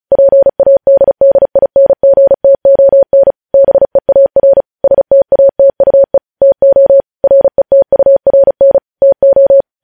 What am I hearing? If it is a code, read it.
Morse code, heard as a keyed tone, about 36 wpm, 550 Hz. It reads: PADDINGTON BEAR STATUE TO RETURN TO
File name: podcast / News headlines-1x-35x35